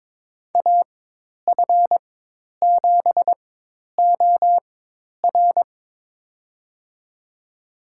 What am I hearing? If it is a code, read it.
Morse code, heard as a keyed tone, about 22 wpm, spaced as for 10 wpm, 700 Hz. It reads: AF7OR
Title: USA Call Signs - 5 Characters - 4x Character Spacing - 22wpm